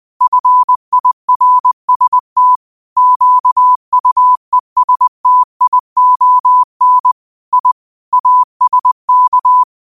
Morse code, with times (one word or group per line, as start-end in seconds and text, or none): 0.2-2.6 FIRST
3.0-7.1 QUESTION
7.5-7.7 I
8.1-9.6 ASK